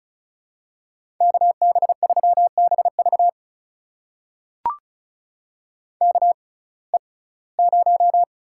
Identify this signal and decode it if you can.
Morse code, heard as a keyed tone, about 36 wpm, spaced as for 21 wpm, 700 Hz. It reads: KB3BV KE0
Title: USA Call Signs - 6x Character Spacing - 35wpm